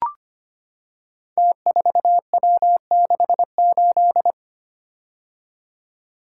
Morse code, heard as a keyed tone, about 25 wpm, 700 Hz. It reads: T4W68
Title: Code Groups - 5 Letter:Numbers - 25wpm